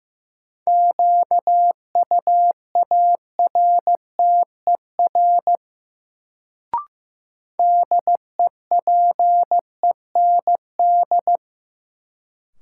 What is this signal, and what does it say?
Morse code, heard as a keyed tone, about 15 wpm, 700 Hz. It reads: QUARTER DEPEND
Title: Sentences from Top 1300 Words - New Word Drill - 15wpm